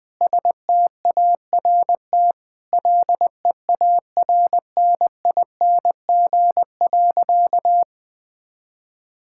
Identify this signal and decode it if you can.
Morse code, heard as a keyed tone, about 20 wpm, 700 Hz. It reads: START LEARNING.
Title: Sentences with 6-Word Phrases - 20wpm